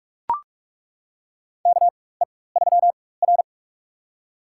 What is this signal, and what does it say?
Morse code, heard as a keyed tone, about 42 wpm, spaced as for 20 wpm, 700 Hz. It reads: KE3R
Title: USA Call Signs - 4 Characters - 4x Character Spacing - 45wpm